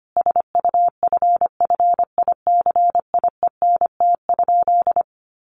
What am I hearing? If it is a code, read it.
Morse code, heard as a keyed tone, about 25 wpm, 700 Hz. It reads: SUFFICIENT?
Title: Sentences from Top 1400 Words 25wpm